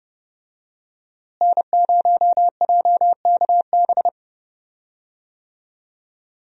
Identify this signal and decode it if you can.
Morse code, heard as a keyed tone, about 31 wpm, 700 Hz. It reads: N0JKB
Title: USA Call Signs - 5 Characters - Repeated 2x - 30wpm